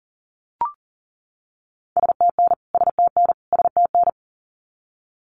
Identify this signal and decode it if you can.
Morse code, heard as a keyed tone, about 40 wpm, 700 Hz. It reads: STN STN STN
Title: Abbreviations Repeated 3x - 40wpm